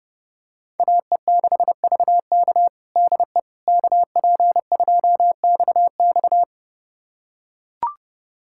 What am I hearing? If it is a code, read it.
Morse code, heard as a keyed tone, about 31 wpm, 700 Hz. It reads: AE6VK DE KP2XX
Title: USA Callsign DE Callsign - 5 Characters - 30wpm